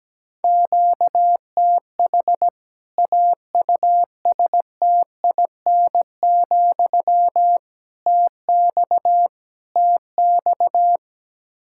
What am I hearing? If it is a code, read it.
Morse code, heard as a keyed tone, about 17 wpm, 700 Hz. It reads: QTH AUSTIN, TX TX
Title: K1USN SST QSO Practice 17wpm